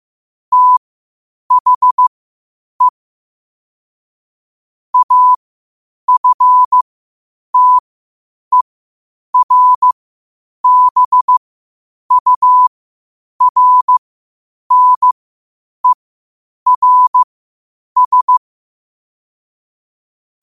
Morse code, heard as a keyed tone, about 15 wpm, spaced as for 8 wpm, 1000 Hz. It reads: THE AFTERBURNERS
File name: qotdfiles / 8wpm